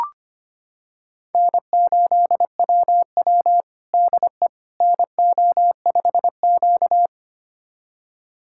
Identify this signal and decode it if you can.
Morse code, heard as a keyed tone, about 25 wpm, 700 Hz. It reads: N8WW DE NO5Q
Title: USA Callsign DE Callsign - 4 Characters - 25wpm